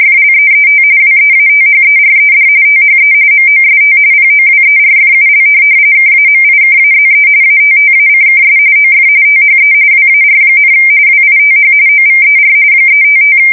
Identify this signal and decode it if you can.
RTTY, about 45.45 baud. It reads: FLEET WEATHERED THE MOST POWERFUL KAMIKAZE ATTACKS OF THE WAR, WITH OVER 355
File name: fsk170 / USS-New-Jersey